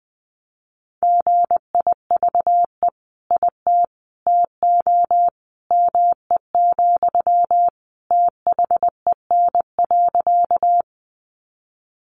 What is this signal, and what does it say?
Morse code, heard as a keyed tone, about 20 wpm, 700 Hz. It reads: GIVE IT TO ME, THEN.
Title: Sentences from Top 200 Words 20wpm